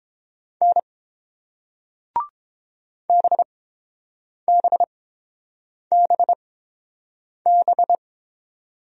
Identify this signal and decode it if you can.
Morse code, heard as a keyed tone, about 30 wpm, spaced as for 7 wpm, 700 Hz. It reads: N BBBB